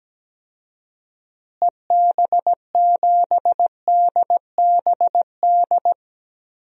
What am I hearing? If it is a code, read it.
Morse code, heard as a keyed tone, about 17 wpm, 700 Hz. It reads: EB7DBD
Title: DXCC-100 Call Signs - 6 Characters - Repeated 3x - 17wpm